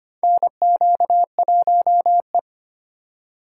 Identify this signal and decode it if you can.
Morse code, heard as a keyed tone, about 25 wpm, 700 Hz. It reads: NQ1E
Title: USA Call Signs - 4 Characters - 4x Character Spacing - 25wpm